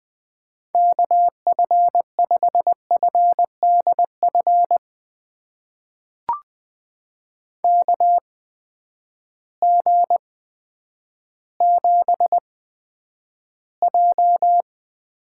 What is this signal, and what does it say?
Morse code, heard as a keyed tone, about 20 wpm, spaced as for 10 wpm, 700 Hz. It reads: KF5FDF KG7J
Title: USA Call Signs - 8x Character Spacing - 20wpm